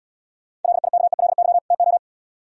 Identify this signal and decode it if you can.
Morse code, heard as a keyed tone, about 77 wpm, 700 Hz. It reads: LIKELY TO